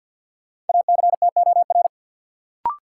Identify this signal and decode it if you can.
Morse code, heard as a keyed tone, about 53 wpm, 700 Hz. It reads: ACTOR
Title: Sentences from Top 1500 Words - New Word Drill - 50wpm